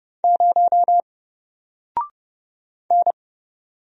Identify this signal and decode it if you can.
Morse code, heard as a keyed tone, about 30 wpm, 700 Hz. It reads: 0 N
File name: Single-Letter-Number-30wpm